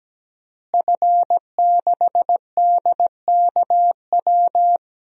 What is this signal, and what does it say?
Morse code, heard as a keyed tone, about 17 wpm, 700 Hz. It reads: F6DKW
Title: DXCC-100 Call Signs - 5 Characters - Repeated 3x - 17wpm